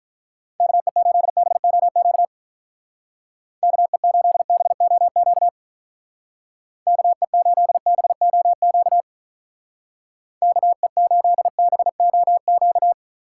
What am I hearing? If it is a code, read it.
Morse code, heard as a keyed tone, about 43 wpm, 700 Hz. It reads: KE8BOQ KE8BOQ KE8BOQ KE8BOQ
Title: Speed Racing - USA 2x3 Call Signs 35wpm